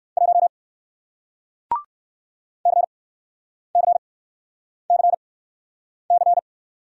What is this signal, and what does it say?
Morse code, heard as a keyed tone, about 57 wpm, 700 Hz. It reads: 1 C C C C